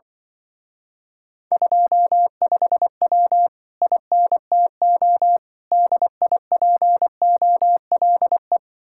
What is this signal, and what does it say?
Morse code, heard as a keyed tone, about 25 wpm, 700 Hz. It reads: 25W INTO DIPOLE
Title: Speed-Racing Common QSO Phrases - 20wpm